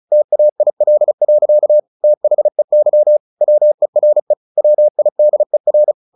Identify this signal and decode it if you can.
Morse code, heard as a keyed tone, about 34 wpm, 600 Hz. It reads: TAIL. THEY WERE WIDER
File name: the old man and the sea 2 35wpm